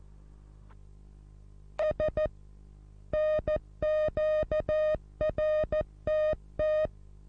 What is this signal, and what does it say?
Morse code, heard as a keyed tone, about 17 wpm, 620 Hz. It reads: S NQRTT
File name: repos / Morsekurs / DJ4UF Morsen27